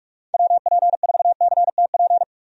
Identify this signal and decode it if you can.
Morse code, heard as a keyed tone, about 46 wpm, 700 Hz. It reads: WP4CTP
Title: USA Call Signs - 6 Characters - 45wpm